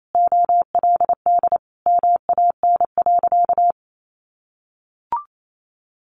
Morse code, heard as a keyed tone, about 28 wpm, 700 Hz. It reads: OLD MAN.
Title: Sentences from Top 1500 Words 28wpm